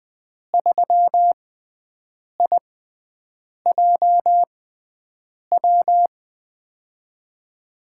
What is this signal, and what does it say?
Morse code, heard as a keyed tone, about 20 wpm, spaced as for 7 wpm, 700 Hz. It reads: 3IJW
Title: USA Call Signs - 5 Characters - 6x Character Spacing - 20wpm